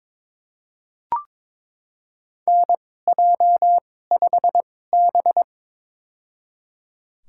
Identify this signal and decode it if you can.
Morse code, heard as a keyed tone, about 22 wpm, spaced as for 16 wpm, 700 Hz. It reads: NJ5B
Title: USA Call Signs - 4 Characters - 2x Character Spacing - 22wpm